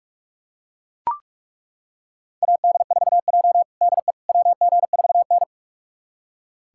Morse code, heard as a keyed tone, about 41 wpm, 700 Hz. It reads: AD4J DE WG4N
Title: USA Callsign DE Callsign - 4 Characters - 45wpm